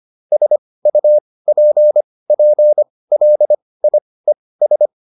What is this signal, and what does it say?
Morse code, heard as a keyed tone, about 25 wpm, spaced as for 18 wpm, 600 Hz. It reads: SUPPLIES